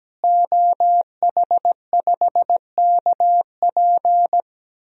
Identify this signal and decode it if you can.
Morse code, heard as a keyed tone, about 17 wpm, 700 Hz. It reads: OH5KP
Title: DXCC-100 Call Signs - 5 Characters - Repeated 2x - 17wpm